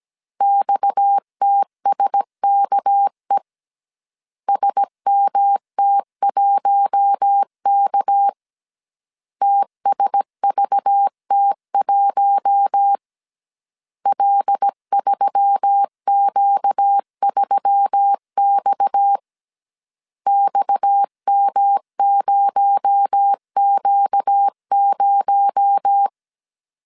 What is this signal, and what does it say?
Morse code, heard as a keyed tone, about 17 wpm, 800 Hz. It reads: XTSKE SMT1K TSVT1 L3Q3X XM0Q0
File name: 16wpm